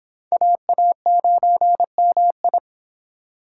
Morse code, heard as a keyed tone, about 24 wpm, 700 Hz. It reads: AA9MI